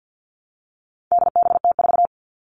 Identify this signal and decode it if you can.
Morse code, heard as a keyed tone, about 50 wpm, 700 Hz. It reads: DBT4
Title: All Cut Numbers - 4-Digit 50wpm